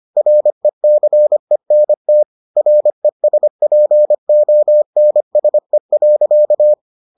Morse code, heard as a keyed tone, about 25 wpm, 600 Hz. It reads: RECENT RESPONSE.